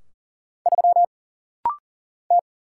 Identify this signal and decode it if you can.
Morse code, heard as a keyed tone, about 40 wpm, 700 Hz. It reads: 3 T
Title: Single Letter-Number - Rapid-Fire 40wpm